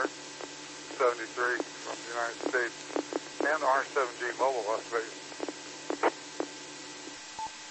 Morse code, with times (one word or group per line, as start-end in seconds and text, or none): none